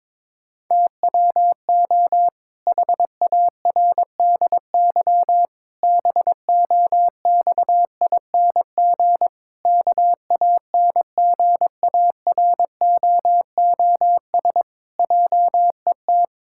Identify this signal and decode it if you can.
Morse code, heard as a keyed tone, about 22 wpm, 700 Hz. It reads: TWO HARDY BOXING KANGAROOS JET
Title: Pangram Challenge v2 - 22wpm